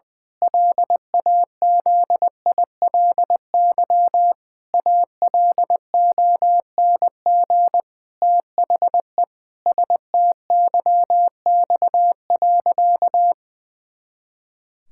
Morse code, heard as a keyed tone, about 21 wpm, 700 Hz. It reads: LAZILY ALONG THE STYX.
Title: Pangram Challenge v2 - 20wpm